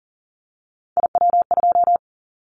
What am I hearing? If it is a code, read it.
Morse code, heard as a keyed tone, about 40 wpm, 700 Hz. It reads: IW2
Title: Callsign Prefixes for Top 50 DXCC - 40wpm